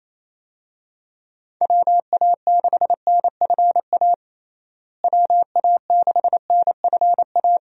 Morse code, heard as a keyed tone, about 28 wpm, 700 Hz. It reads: WA6NFA WA6NFA
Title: USA Call Signs - 6 Characters - Repeated 2x - 28wpm